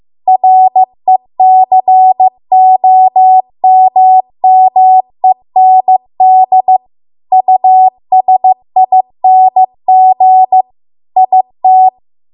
Morse code, heard as a keyed tone, about 15 wpm, 750 Hz. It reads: RECOMMEND USING IT